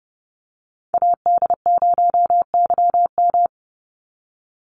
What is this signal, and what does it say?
Morse code, heard as a keyed tone, about 30 wpm, 700 Hz. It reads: AD0YM
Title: NA SOTA Chaser Callsigns 30wpm